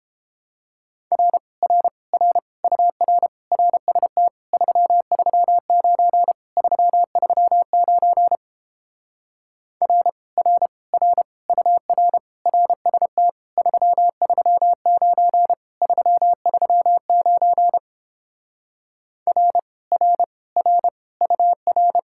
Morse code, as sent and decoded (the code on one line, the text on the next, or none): .-. / .-. / .-. / ..- .-. / .-. ... - / ...-- ...-- ----. / ...-- ...-- ----. / .-. / .-. / .-. / ..- .-. / .-. ... - / ...-- ...-- ----. / ...-- ...-- ----. / .-. / .-. / .-. / ..- .-.
R R R UR RST 339 339 R R R UR RST 339 339 R R R UR